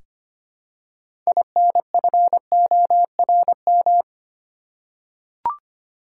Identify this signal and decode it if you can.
Morse code, heard as a keyed tone, about 25 wpm, 700 Hz. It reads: INFORM
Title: Six Letter Words - 25wpm